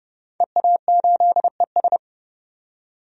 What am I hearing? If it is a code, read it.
Morse code, heard as a keyed tone, about 30 wpm, 700 Hz. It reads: EA8ES
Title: DXCC-100 Call Signs - 5 Characters - 30wpm